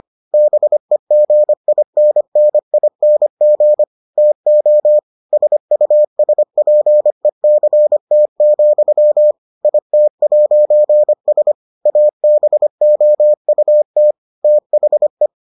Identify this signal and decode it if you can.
Morse code, heard as a keyed tone, about 25 wpm, 600 Hz. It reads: BEGINNING TO SUSPECT, IT'S ABOUT THE